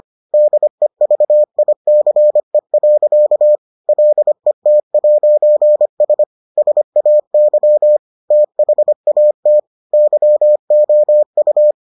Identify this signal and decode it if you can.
Morse code, heard as a keyed tone, about 25 wpm, 600 Hz. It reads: DEVICE. LET'S SAY THAT YOU